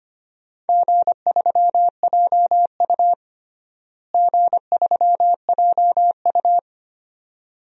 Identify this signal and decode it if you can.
Morse code, heard as a keyed tone, about 25 wpm, 700 Hz. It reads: G3JU G3JU
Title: DXCC-100 Call Signs - 4 Characters - Repeated 2x - 25wpm